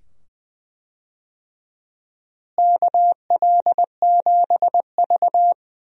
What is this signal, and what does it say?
Morse code, heard as a keyed tone, about 20 wpm, 700 Hz. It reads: KL7V
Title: DXCC-100 Call Signs - 4 Characters - Repeated 2x - 20wpm